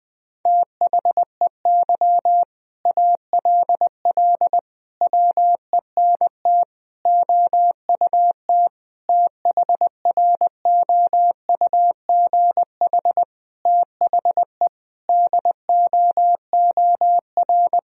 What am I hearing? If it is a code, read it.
Morse code, heard as a keyed tone, about 20 wpm, 700 Hz. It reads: THEY ALL WENT OUT THROUGH THE DOOR
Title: Sentences from Top 1000 Words - Review Part 1 - 20wpm